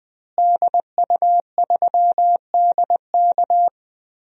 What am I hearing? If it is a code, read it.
Morse code, heard as a keyed tone, about 20 wpm, 700 Hz. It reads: DU3DK
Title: DXCC-100 Call Signs - 5 Characters - Repeated 3x - 20wpm